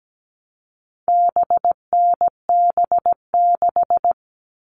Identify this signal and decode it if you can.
Morse code, heard as a keyed tone, about 17 wpm, 700 Hz. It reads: BNB6
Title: All Cut Numbers - 4-Digit 17wpm